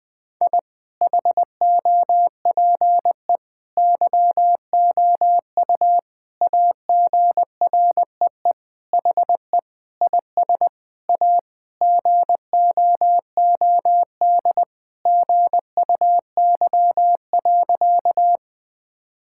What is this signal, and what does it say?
Morse code, heard as a keyed tone, about 20 wpm, 700 Hz. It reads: I HOPE YOU AGREE HE IS A GOOD GUY.
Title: Sentences with 8-Word Phrases - Spoken Beginning - 20wpm